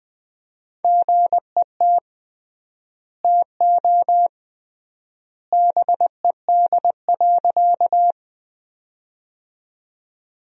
Morse code, heard as a keyed tone, about 20 wpm, 700 Hz. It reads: GET TO BED.
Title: 3x Word-Spacing - Sentences from Top 500 Words - 20wpm